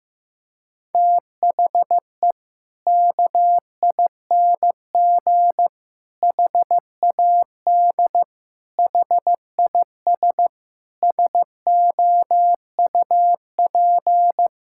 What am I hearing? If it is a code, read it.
Morse code, heard as a keyed tone, about 15 wpm, 700 Hz. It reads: THE KING HAD HIS SOUP
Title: Sentences from Top 1000 Words - Review Part 6 - 15wpm